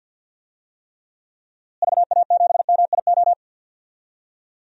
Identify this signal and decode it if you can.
Morse code, heard as a keyed tone, about 52 wpm, 700 Hz. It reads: VA7MIO
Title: DXCC-100 Call Signs - 6 Characters - Repeated 3x - 50wpm